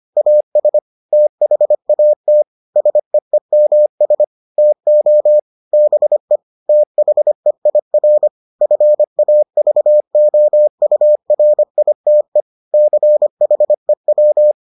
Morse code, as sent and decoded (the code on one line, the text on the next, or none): .- ... / - .... .- - / ... . -- ... / - --- / -... / - .... .. .-. / ..-. .- ...- --- ..- .-. .. - . / -.-. .... .--
AS THAT SEEMS TO BE THEIR FAVOURITE CHEW